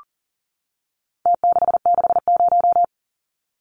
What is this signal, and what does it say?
Morse code, heard as a keyed tone, about 40 wpm, 700 Hz. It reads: T660
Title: Numbers - Contest Style - 4-Digit 40wpm